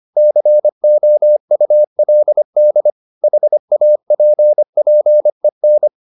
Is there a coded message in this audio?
Yes: Morse code from 0.2 to 5.9 s, heard as a keyed tone, about 25 wpm, 600 Hz. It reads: COULD HAPPEN